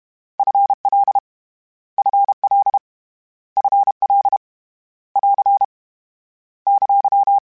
Morse code, heard as a keyed tone, about 32 wpm, 800 Hz. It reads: F L F L F L <AR> !
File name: Contrasts / Contrasts-mp3s / 2LtrContrasts @30